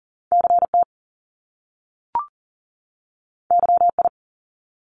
Morse code, heard as a keyed tone, about 41 wpm, 700 Hz. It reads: CT YI
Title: Top Two Letter Combinations in Words 40wpm